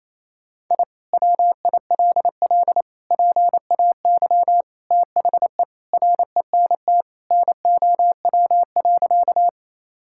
Morse code, heard as a keyed tone, about 28 wpm, 700 Hz. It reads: I WILL PAY THE RENT NOW.
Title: Sentences from Top 1000 Words - Review Part 5 - 28wpm